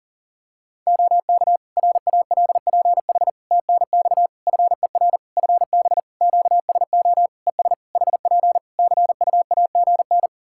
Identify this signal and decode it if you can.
Morse code, heard as a keyed tone, about 41 wpm, 700 Hz. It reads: OK RALPH TNX FER FB QSO ES HP CUAGN
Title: Practice QSO - Exchange 4 - Ending - 40wpm